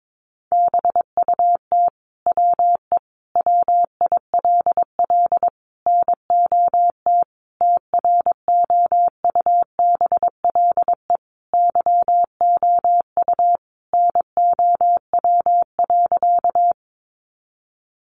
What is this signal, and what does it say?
Morse code, heard as a keyed tone, about 22 wpm, 700 Hz. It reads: BUT WE WILL NOT TROUBLE YOU NOW.
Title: Sentences from Top 1200 Words 22wpm